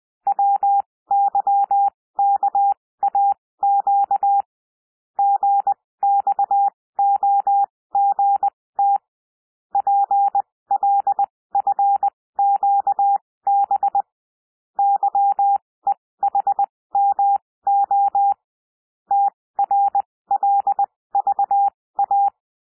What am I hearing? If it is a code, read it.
Morse code, heard as a keyed tone, about 20 wpm, spaced as for 16 wpm, 800 Hz. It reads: WYKAQ GXOGT PLFQB YEHMO TRLVA
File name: cwmp3 / L1 SOUND LETTERS 20 wpm